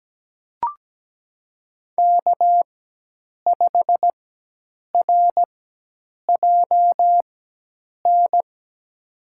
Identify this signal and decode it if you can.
Morse code, heard as a keyed tone, about 17 wpm, spaced as for 8 wpm, 700 Hz. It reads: K5RJN
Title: USA Call Signs - 5 Characters - 4x Character Spacing - 17wpm